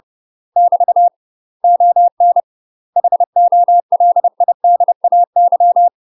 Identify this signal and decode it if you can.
Morse code, heard as a keyed tone, about 30 wpm, 700 Hz. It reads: <BT> ON HOLIDAY
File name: UK MASTER QSOs TRAINING / 30WPM / UK MASTER QSOs 0008